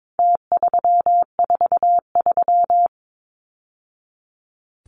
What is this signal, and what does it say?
Morse code, heard as a keyed tone, about 22 wpm, 700 Hz. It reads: T343
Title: Numbers - Contest Style - 4-Digit 22wpm